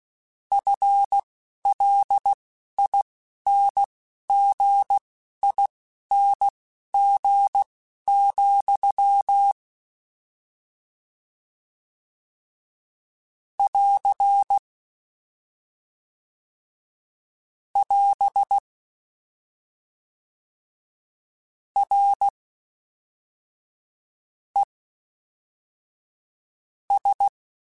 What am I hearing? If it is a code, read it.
Morse code, heard as a keyed tone, about 16 wpm, spaced as for 11 wpm, 800 Hz. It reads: FLINGING, <AR> <AS> R E S
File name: Combinations / Combinations-mp3s / Combinations@10